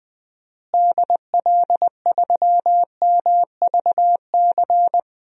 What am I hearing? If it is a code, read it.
Morse code, heard as a keyed tone, about 20 wpm, 700 Hz. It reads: DL3MVC
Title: DXCC-100 Call Signs - 6 Characters - Repeated 3x - 20wpm